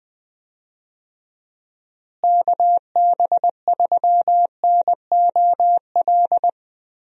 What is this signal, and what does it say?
Morse code, heard as a keyed tone, about 21 wpm, 700 Hz. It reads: KB3NOL
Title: USA Call Signs - 6 Characters - Repeated 2x - 20wpm